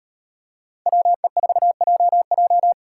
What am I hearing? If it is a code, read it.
Morse code, heard as a keyed tone, about 38 wpm, 700 Hz. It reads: WE4JJ